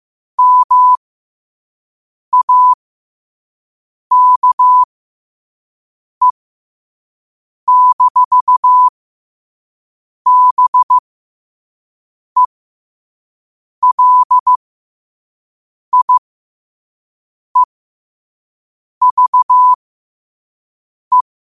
Morse code, heard as a keyed tone, about 15 wpm, spaced as for 5 wpm, 1000 Hz. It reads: MAKE-BELIEVE
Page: Quotes for Fri, 15 Aug 2025 in Morse Code at 5 words per minute.